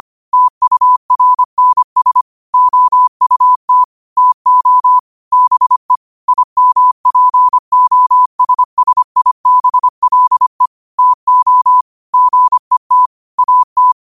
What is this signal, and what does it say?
Morse code, heard as a keyed tone, about 25 wpm, 1000 Hz. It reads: TURNS OUT TO BE IMPOSSIBLE TO GET AT
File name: qotdfiles / 25wpm